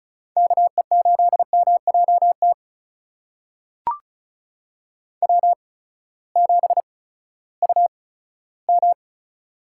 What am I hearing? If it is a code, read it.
Morse code, heard as a keyed tone, about 36 wpm, spaced as for 18 wpm, 700 Hz. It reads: KE8MJT W7UM
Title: USA Call Signs - 8x Character Spacing - 35wpm